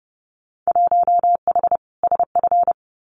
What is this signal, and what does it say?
Morse code, heard as a keyed tone, about 30 wpm, 700 Hz. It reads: 1H SF
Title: Winter Field Day Exchanges 30wpm